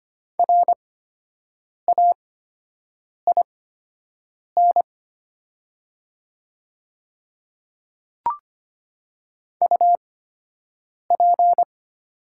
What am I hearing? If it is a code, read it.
Morse code, heard as a keyed tone, about 25 wpm, spaced as for 7 wpm, 700 Hz. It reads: RAIN UP
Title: 8x Character Spacing - Top 300 Words - 25wpm